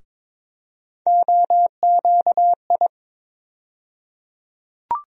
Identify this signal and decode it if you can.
Morse code, heard as a keyed tone, about 22 wpm, 700 Hz. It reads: OQI
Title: USA Call Sign Postfixes - 3 Letters 22wpm